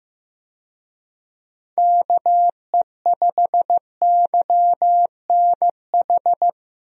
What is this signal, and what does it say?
Morse code, heard as a keyed tone, about 15 wpm, 700 Hz. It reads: KE5YNH
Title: USA Call Signs - 6 Characters - Repeated 3x - 15wpm